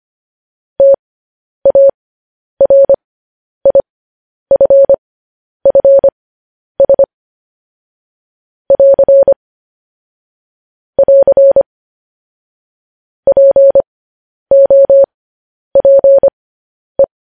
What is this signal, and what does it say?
Morse code, heard as a keyed tone, about 25 wpm, spaced as for 10 wpm, 550 Hz. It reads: TARIFFS <AR> <AR> POPE
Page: news headlines in morse code at 10 wpm